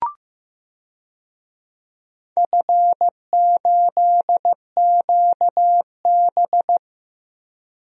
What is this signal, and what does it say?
Morse code, heard as a keyed tone, about 15 wpm, 700 Hz. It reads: F8QB
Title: DXCC-100 Call Signs - 4 Characters - Repeated 3x - 15wpm